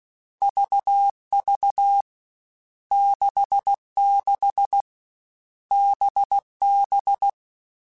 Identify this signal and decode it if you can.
Morse code, heard as a keyed tone, about 16 wpm, 800 Hz. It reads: VV 66 BB
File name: Groups@13 / Groups@13 mp3s / VB46@13